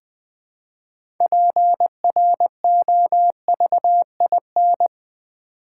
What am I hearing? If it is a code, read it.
Morse code, heard as a keyed tone, about 20 wpm, 700 Hz. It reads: PROVIN
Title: Build A Word - 2 Letters At A Time - 8-Letter Words - 20wpm